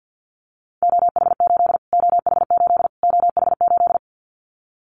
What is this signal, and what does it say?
Morse code, heard as a keyed tone, about 50 wpm, 700 Hz. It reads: OH8 OH8 OH8
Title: Callsign Prefixes for Top 25 DXCC - Repeated 3x - 50wpm